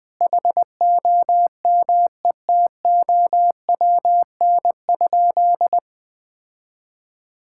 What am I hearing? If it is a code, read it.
Morse code, heard as a keyed tone, about 20 wpm, 700 Hz. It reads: HOMETOWN?
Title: Sentences with 8-Word Phrases - 20wpm